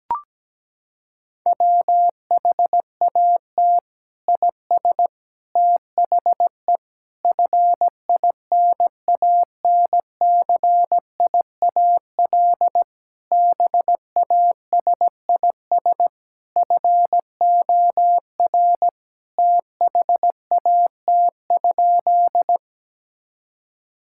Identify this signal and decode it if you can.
Morse code, heard as a keyed tone, about 17 wpm, 700 Hz. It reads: WHAT IS THE FINANCIAL BASIS FOR THAT?
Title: Sentences from Top 1000 Words - Review Part 5 - 17wpm